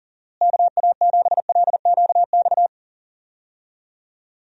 Morse code, heard as a keyed tone, about 39 wpm, 700 Hz. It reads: KA7LQX
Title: USA Call Signs - 6 Characters - Repeated 2x - 40wpm